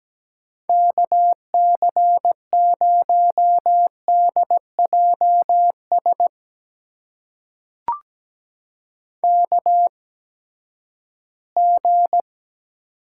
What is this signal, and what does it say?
Morse code, heard as a keyed tone, about 17 wpm, spaced as for 12 wpm, 700 Hz. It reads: KC0DJS KG
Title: USA Call Signs - 6 Characters - 8x Character Spacing - 17wpm